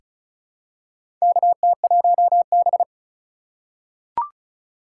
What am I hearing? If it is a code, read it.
Morse code, heard as a keyed tone, about 34 wpm, 700 Hz. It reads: KT1B
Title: K1USN SST QSO Practice - 35wpm